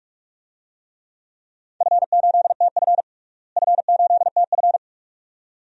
Morse code, heard as a keyed tone, about 41 wpm, 700 Hz. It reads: F8TF F8TF
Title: DXCC-100 Call Signs - 4 Characters - Repeated 2x - 45wpm